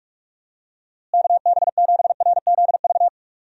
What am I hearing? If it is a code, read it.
Morse code, heard as a keyed tone, about 45 wpm, 700 Hz. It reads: KB7RZV
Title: USA Call Signs - 6 Characters - Repeated 2x - 45wpm